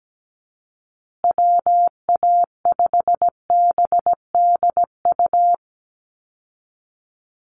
Morse code, heard as a keyed tone, about 17 wpm, 700 Hz. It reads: WA5BDU
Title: Call Signs Repeated 2x - 17wpm